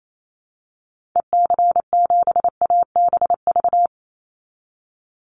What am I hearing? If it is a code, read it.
Morse code, heard as a keyed tone, about 28 wpm, 700 Hz. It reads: EC7ABV
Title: Call Signs - 28wpm